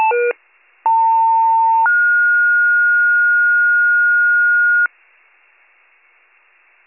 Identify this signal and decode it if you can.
Two-tone page, heard as a one-second tone, then a three-second tone, about 900 then 1430 Hz. It is heard from 0.9 to 4.9 s.